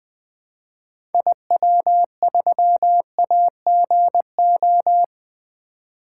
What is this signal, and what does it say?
Morse code, heard as a keyed tone, about 21 wpm, 700 Hz. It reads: IW3AGO
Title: 2023 SOTA Activator Call Signs - 20wpm